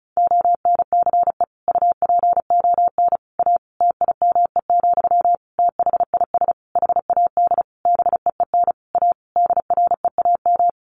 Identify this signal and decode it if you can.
Morse code, heard as a keyed tone, about 34 wpm, 700 Hz. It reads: ONCE UPON A TIME, THIS HAD BEEN A DREAM
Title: Sentences from Top 1500 Words 35wpm